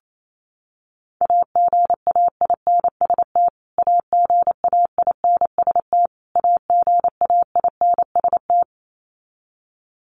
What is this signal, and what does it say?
Morse code, heard as a keyed tone, about 28 wpm, 700 Hz. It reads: AGAINST AGAINST AGAINST
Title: Top 500 Words Repeated 28wpm